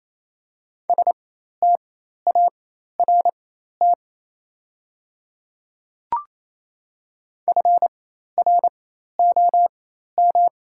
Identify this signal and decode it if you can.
Morse code, heard as a keyed tone, about 28 wpm, spaced as for 13 wpm, 700 Hz. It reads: START FROM
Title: 4x Character Spacing - Top 500 Words - 28wpm